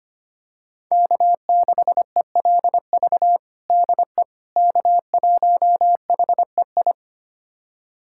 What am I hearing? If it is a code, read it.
Morse code, heard as a keyed tone, about 25 wpm, 700 Hz. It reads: K6ELV DE K1HEI
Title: USA Callsign DE Callsign - 5 Characters - 25wpm